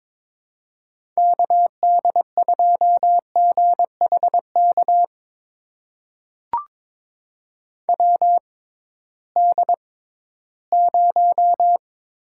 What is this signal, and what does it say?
Morse code, heard as a keyed tone, about 22 wpm, spaced as for 14 wpm, 700 Hz. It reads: KD2GHK WD0
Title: USA Call Signs - 6 Characters - 6x Character Spacing - 22wpm